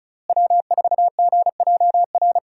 Morse code, heard as a keyed tone, about 36 wpm, 700 Hz. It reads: W4GJR
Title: USA Call Signs - 5 Characters - Repeated 3x - 35wpm